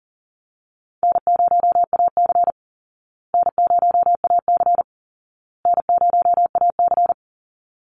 Morse code, heard as a keyed tone, about 40 wpm, 700 Hz. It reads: N0AC N0AC N0AC
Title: Easy Call Signs Repeated 3x - 40wpm